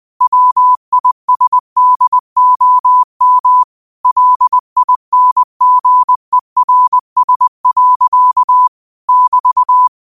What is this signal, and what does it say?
Morse code, heard as a keyed tone, about 20 wpm, 1000 Hz. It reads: WISDOM LINGERS. <BT>
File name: qotdfiles / 20wpm